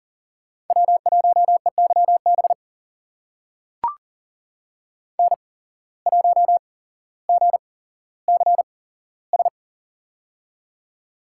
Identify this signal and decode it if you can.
Morse code, heard as a keyed tone, about 40 wpm, spaced as for 17 wpm, 700 Hz. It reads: W1EYB N1GCS
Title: USA Call Signs - 5 Characters - 8x Character Spacing - 40wpm